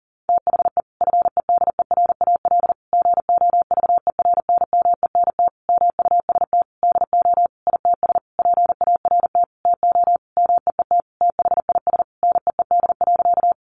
Morse code, heard as a keyed tone, about 40 wpm, 700 Hz. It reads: THE FEDERAL GOVERNMENT MUST DO ITS PART TO MEET THIS NEED.
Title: Sentences from Top 1100 Words 40wpm